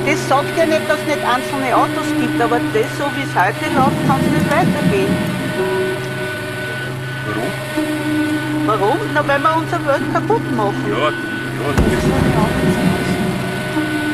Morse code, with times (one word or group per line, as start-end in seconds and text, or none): none